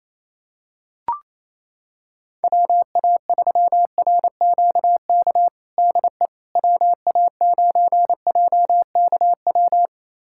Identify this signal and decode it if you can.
Morse code, heard as a keyed tone, about 28 wpm, 700 Hz. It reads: WA3RQK DE WA9JKW
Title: USA Callsign DE Callsign - 6 Characters - 28wpm